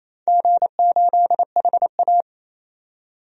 Morse code, heard as a keyed tone, about 28 wpm, 700 Hz. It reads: G8HA
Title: DXCC-100 Call Signs - 4 Characters - 28wpm